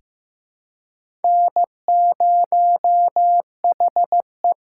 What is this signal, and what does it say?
Morse code, heard as a keyed tone, about 15 wpm, 700 Hz. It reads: N0HE
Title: USA Call Signs - 4 Characters - 3x Character Spacing - 15wpm